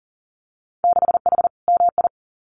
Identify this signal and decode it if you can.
Morse code, heard as a keyed tone, about 40 wpm, 700 Hz. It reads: 6H MI
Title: Winter Field Day Exchanges Encore 40wpm